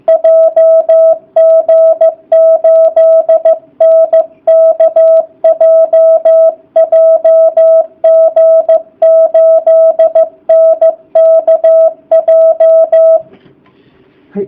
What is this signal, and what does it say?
Morse code, heard as a keyed tone, about 15 wpm, 660 Hz. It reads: JG8NKJ JG8NKJ